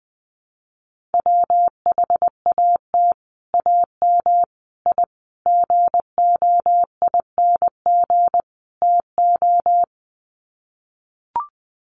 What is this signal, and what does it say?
Morse code, heard as a keyed tone, about 19 wpm, 700 Hz. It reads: WHAT AM I GOING TO
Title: Sets of 5 Words Encore 20wpm